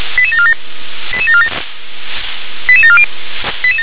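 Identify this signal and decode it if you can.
Five-tone selective call (ZVEI), heard as quick runs of five tones, tones of about 70 ms. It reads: 88747 88740
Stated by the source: ZVEI five-tone selective call received by SDR.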